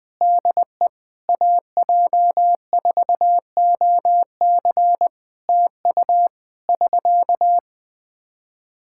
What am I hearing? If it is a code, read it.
Morse code, heard as a keyed tone, about 20 wpm, 700 Hz. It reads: DE AJ4OC TU <SK>
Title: Practice QSO - Exchange 4 - Ending - 20wpm